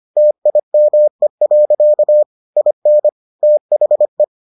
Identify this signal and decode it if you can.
Morse code, heard as a keyed tone, about 25 wpm, 600 Hz. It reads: TIME. IN THE